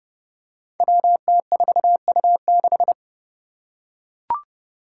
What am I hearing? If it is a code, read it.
Morse code, heard as a keyed tone, about 29 wpm, 700 Hz. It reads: WT4U6
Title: Five Challenging Letter-Number Combinations 30wpm